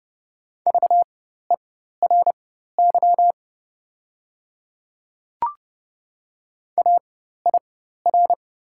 Morse code, heard as a keyed tone, about 30 wpm, spaced as for 14 wpm, 690 Hz. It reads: VERY AIR
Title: 4x Character Spacing - Top 200 Words - 30wpm